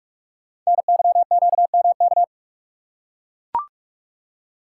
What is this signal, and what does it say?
Morse code, heard as a keyed tone, about 44 wpm, 690 Hz. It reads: NYQMK
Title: Code Groups - 5 Letters - 45wpm